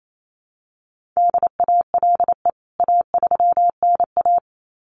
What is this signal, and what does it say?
Morse code, heard as a keyed tone, about 28 wpm, 700 Hz. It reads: DALE A3NA
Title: CWT Exchange 28wpm